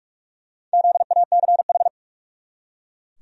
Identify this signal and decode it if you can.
Morse code, heard as a keyed tone, about 44 wpm, 700 Hz. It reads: ZACH
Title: USA Common Names - Max 5 Letters - 45wpm